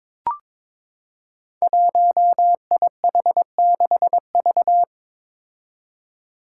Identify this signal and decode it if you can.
Morse code, heard as a keyed tone, about 22 wpm, 700 Hz. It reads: 1IH6V
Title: Five Challenging Letter-Number Combinations 22wpm